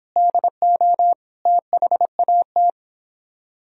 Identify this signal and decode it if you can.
Morse code, heard as a keyed tone, about 29 wpm, 700 Hz. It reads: DO THAT